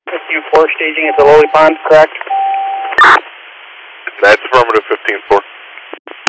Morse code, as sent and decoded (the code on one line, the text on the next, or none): none